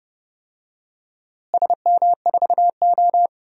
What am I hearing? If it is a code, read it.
Morse code, heard as a keyed tone, about 29 wpm, 700 Hz. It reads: SM4O
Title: DXCC-100 Call Signs - 4 Characters - Repeated 3x - 30wpm